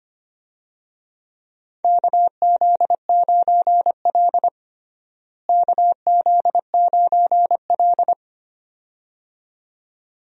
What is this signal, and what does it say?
Morse code, heard as a keyed tone, about 25 wpm, 700 Hz. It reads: KZ9L KZ9L
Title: USA Call Signs - 4 Characters - Repeated 2x - 25wpm